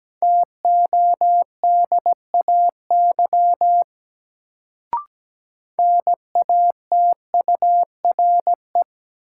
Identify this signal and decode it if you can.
Morse code, heard as a keyed tone, about 17 wpm, 700 Hz. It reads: TODAY NATURE
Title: Sentences from Top 600 Words - New Word Drill - 17wpm